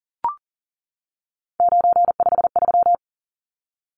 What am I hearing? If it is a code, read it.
Morse code, heard as a keyed tone, about 40 wpm, 700 Hz. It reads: 953
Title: Numbers - 3-Digit 40wpm